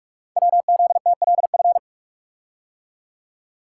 Morse code, heard as a keyed tone, about 44 wpm, 700 Hz. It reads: WZTLF
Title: Code Groups - 5 Letters - 45wpm